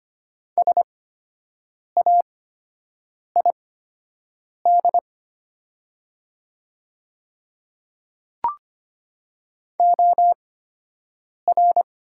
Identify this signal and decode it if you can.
Morse code, heard as a keyed tone, about 25 wpm, spaced as for 7 wpm, 700 Hz. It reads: SAID OR
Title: 8x Character Spacing - Top 100 Words - 25wpm